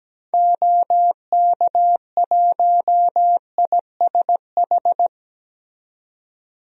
Morse code, heard as a keyed tone, about 17 wpm, 700 Hz. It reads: OK1ISH
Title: DXCC-100 Call Signs - 6 Characters - 17wpm